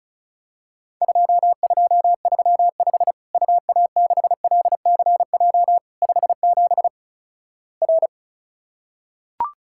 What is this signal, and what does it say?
Morse code, heard as a keyed tone, about 36 wpm, 700 Hz. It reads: 2235 UA6LCJ 57 R
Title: Worked All Europe DX Contest - QTC Exchange - 35wpm